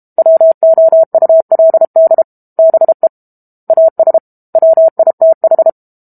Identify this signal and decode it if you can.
Morse code, heard as a keyed tone, about 32 wpm, 650 Hz. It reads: WOULD BE AS WITH
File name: Book Two / 06-Chapter 6 - 32 WPM